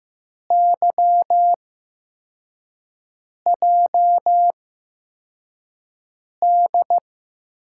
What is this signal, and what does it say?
Morse code, heard as a keyed tone, about 15 wpm, spaced as for 4 wpm, 700 Hz. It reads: YJD
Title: USA Call Signs - 6 Characters - 8x Character Spacing - 15wpm